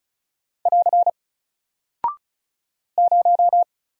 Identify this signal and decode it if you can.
Morse code, heard as a keyed tone, about 36 wpm, 700 Hz. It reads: <AR> 0
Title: Single Character (Letter:Number:Punctuation:Prosign) 35wpm